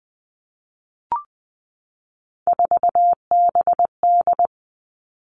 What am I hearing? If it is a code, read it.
Morse code, heard as a keyed tone, about 20 wpm, 700 Hz. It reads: 4BD
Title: Three Challenging Letter-Number Combinations 20wpm